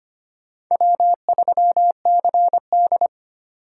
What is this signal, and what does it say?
Morse code, heard as a keyed tone, about 31 wpm, 700 Hz. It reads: W3CD
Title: Speed Racing - USA Call Signs 25wpm